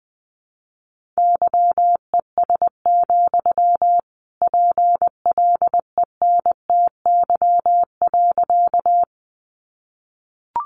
Morse code, heard as a keyed tone, about 20 wpm, 700 Hz. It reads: YES, PLENTY.
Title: Sentences from Top 1300 Words 20wpm